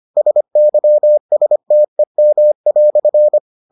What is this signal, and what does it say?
Morse code, heard as a keyed tone, about 25 wpm, 600 Hz. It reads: SYSTEM"